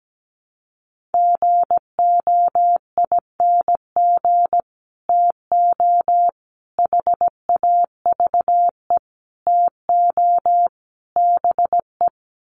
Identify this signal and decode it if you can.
Morse code, heard as a keyed tone, about 17 wpm, 700 Hz. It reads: GOING TO HAVE TO BE
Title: Sets of 5 Words 17wpm